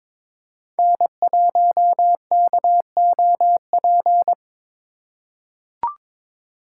Morse code, heard as a keyed tone, about 22 wpm, 700 Hz. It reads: N1KOP
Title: USA Call Signs - 2x Character Spacing - 22wpm